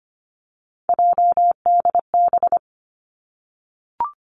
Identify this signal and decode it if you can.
Morse code, heard as a keyed tone, about 25 wpm, 700 Hz. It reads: JDB
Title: Three Challenging Letter-Number Combinations 25wpm